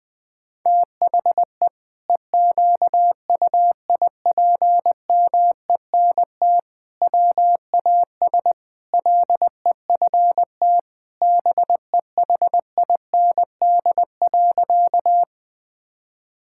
Morse code, heard as a keyed tone, about 20 wpm, 700 Hz. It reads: THE EQUIPMENT WAS LEFT BEHIND.
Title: Sentences from Top 1000 Words - Review Part 5 - 20wpm